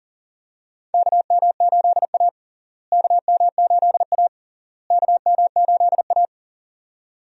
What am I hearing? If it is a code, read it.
Morse code, heard as a keyed tone, about 37 wpm, 700 Hz. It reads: KM8A KM8A KM8A
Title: USA Call Signs - 4 Characters - Repeated 3x - 40wpm